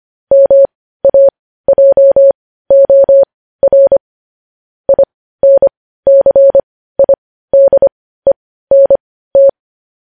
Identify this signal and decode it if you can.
Morse code, heard as a keyed tone, about 25 wpm, spaced as for 15 wpm, 550 Hz. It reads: MAJOR INCIDENT
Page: News Headlines in Morse Code at 15 WPM